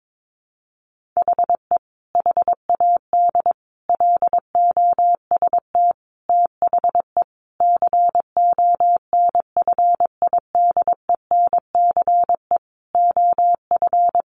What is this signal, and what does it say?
Morse code, heard as a keyed tone, about 22 wpm, 700 Hz. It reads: HE HAD LOST THE CONFIDENCE OF
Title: Sentences from Top 1500 Words 22wpm